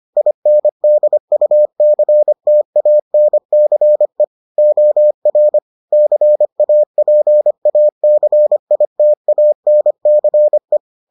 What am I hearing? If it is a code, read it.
Morse code, heard as a keyed tone, about 25 wpm, 600 Hz. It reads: INDUCTANCE OR CAPACITANCE